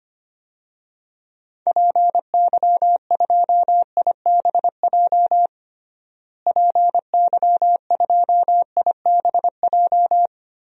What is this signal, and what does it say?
Morse code, heard as a keyed tone, about 25 wpm, 700 Hz. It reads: PY2IBJ PY2IBJ
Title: DXCC-100 Call Signs - 6 Characters - Repeated 3x - 25wpm